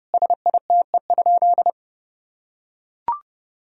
Morse code, heard as a keyed tone, about 32 wpm, 700 Hz. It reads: SITE?
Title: Sentences from Top 1000 Words - Review Part 4 - 30wpm